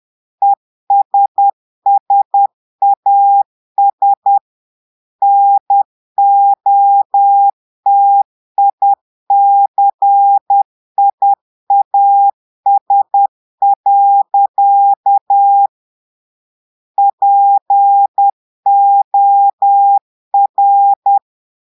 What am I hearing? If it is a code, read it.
Morse code, heard as a keyed tone, about 10 wpm, 800 Hz. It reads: ESSAS NOTICIAS. POR
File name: portugais texte en clair 2 vitesse 10 mots minutes